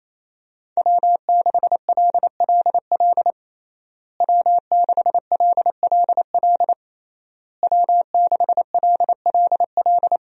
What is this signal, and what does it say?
Morse code, heard as a keyed tone, about 28 wpm, 700 Hz. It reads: W6LLL W6LLL W6LLL
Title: USA Call Signs - 5 Characters - Repeated 3x - 28wpm